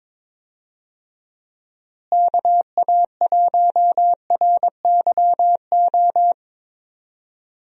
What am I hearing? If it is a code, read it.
Morse code, heard as a keyed tone, about 22 wpm, 700 Hz. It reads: KA1RYO
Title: USA Call Signs - 6 Characters - Repeated 3x - 22wpm